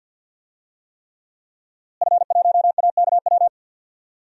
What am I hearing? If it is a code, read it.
Morse code, heard as a keyed tone, about 52 wpm, 700 Hz. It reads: F1AKW
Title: DXCC-100 Call Signs - 5 Characters - Repeated 3x - 50wpm